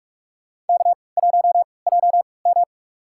Morse code, heard as a keyed tone, about 44 wpm, spaced as for 25 wpm, 700 Hz. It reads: K1JM
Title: USA Call Signs - 4 Characters - 3x Character Spacing - 45wpm